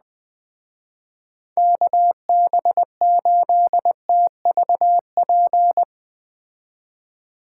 Morse code, heard as a keyed tone, about 20 wpm, 700 Hz. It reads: KB8TVP